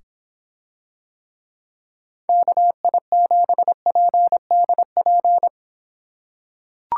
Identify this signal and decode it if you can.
Morse code, heard as a keyed tone, about 25 wpm, 700 Hz. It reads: KI7PDP